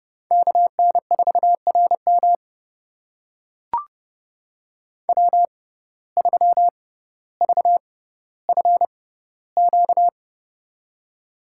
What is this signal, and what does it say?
Morse code, heard as a keyed tone, about 30 wpm, spaced as for 15 wpm, 700 Hz. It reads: KN4RM W3VFQ
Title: USA Call Signs - 5 Characters - 6x Character Spacing - 30wpm